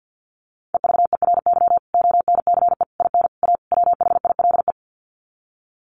Morse code, heard as a keyed tone, about 50 wpm, 700 Hz. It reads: EVERY ONCE IN A WHILE
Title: Sets of 5 Words 50wpm